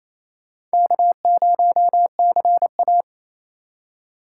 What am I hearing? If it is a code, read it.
Morse code, heard as a keyed tone, about 36 wpm, 700 Hz. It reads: K0CA